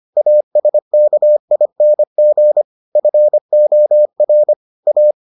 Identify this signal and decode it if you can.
Morse code, heard as a keyed tone, about 25 wpm, 600 Hz. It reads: ASKING FOR A